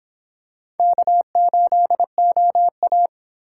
Morse code, heard as a keyed tone, about 25 wpm, 700 Hz. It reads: K8OA